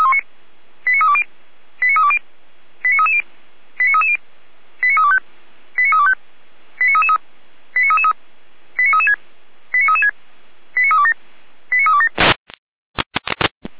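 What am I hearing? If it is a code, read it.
Five-tone selective call (ZVEI), heard as quick runs of five tones, tones of about 70 ms. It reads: 89320 89320 89330 89330 89325 89325 89333 89333 89336 89336 89327 89327
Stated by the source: ZVEI five-tone call received by SDR, fire brigade Orlamünde (feuerwehr-orlamuende).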